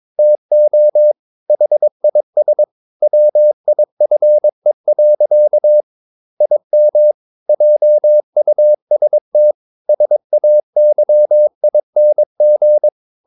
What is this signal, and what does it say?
Morse code, heard as a keyed tone, about 22 wpm, 600 Hz. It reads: TO HIS WIFE. IM JUST SAYING
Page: Romanen Never af Ken Follet på engelsk.